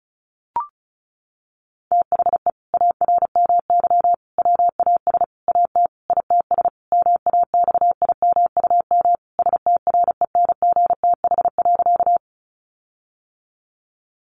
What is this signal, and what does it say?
Morse code, heard as a keyed tone, about 34 wpm, 700 Hz. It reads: THE ARMY WAS AT ITS MAXIMUM STRENGTH.
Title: Sentences from Top 1300 Words 35wpm